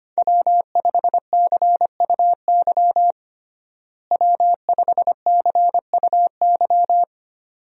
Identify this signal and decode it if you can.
Morse code, heard as a keyed tone, about 25 wpm, 700 Hz. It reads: W5CUY W5CUY
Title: USA Call Signs - 5 Characters - Repeated 2x - 25wpm